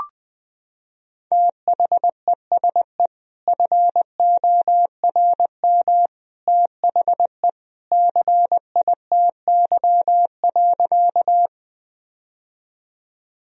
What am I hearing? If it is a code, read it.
Morse code, heard as a keyed tone, about 20 wpm, 700 Hz. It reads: THESE FORM THE CITY.
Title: Sentences from Top 1000 Words - Review Part 1 - 20wpm